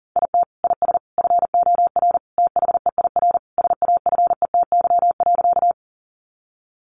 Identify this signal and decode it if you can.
Morse code, heard as a keyed tone, about 40 wpm, 700 Hz. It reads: IT IS FOR THEIR SAFETY.
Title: Sentences from Top 700 Words 40wpm